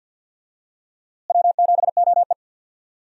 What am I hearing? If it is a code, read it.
Morse code, heard as a keyed tone, about 53 wpm, 700 Hz. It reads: W7OE
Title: USA Call Signs - 4 Characters - Repeated 3x - 50wpm